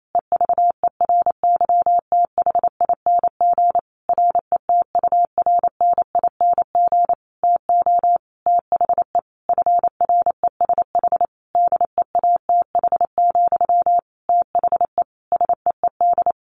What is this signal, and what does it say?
Morse code, heard as a keyed tone, about 28 wpm, 700 Hz. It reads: EVERYTHING RETURNING TO THE FRESH DEATH, THE SEED
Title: A Sound Of Thunder 28wpm